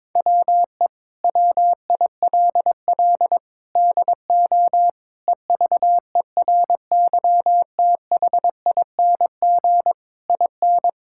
Morse code, heard as a keyed tone, about 22 wpm, 700 Hz. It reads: WE WILL DO EVERYTHING IN
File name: practice / 22wpm / Sets of 8 Words - 22wpm